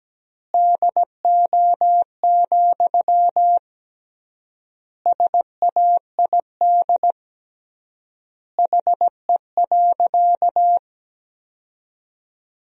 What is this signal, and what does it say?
Morse code, heard as a keyed tone, about 17 wpm, 700 Hz. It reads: DO, SAID HE.
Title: 3x Word-Spacing - Sentences from Top 100 Words - 17wpm